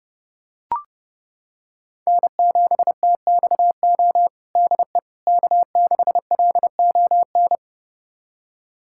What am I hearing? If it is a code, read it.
Morse code, heard as a keyed tone, about 30 wpm, 700 Hz. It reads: N7TXO DE K6LON
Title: USA Callsign DE Callsign - 5 Characters - 30wpm